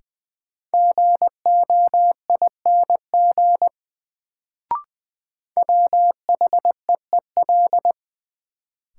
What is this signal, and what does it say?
Morse code, heard as a keyed tone, about 20 wpm, 700 Hz. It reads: GOING WHEEL
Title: Sentences from Top 500 Words - New Word Drill - 20wpm